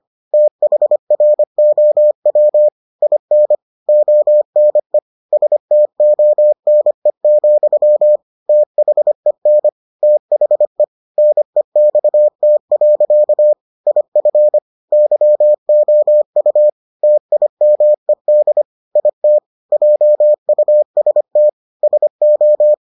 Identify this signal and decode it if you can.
Morse code, heard as a keyed tone, about 25 wpm, 600 Hz. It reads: THROW IN ONE STONE, THEN THE NEXT. IF YOU TIMED IT JUST SO